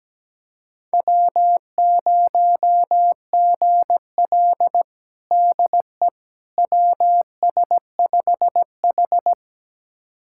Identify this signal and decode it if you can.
Morse code, heard as a keyed tone, about 17 wpm, 700 Hz. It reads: W0GL DE WS5H
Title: USA Callsign DE Callsign - 4 Characters - 17wpm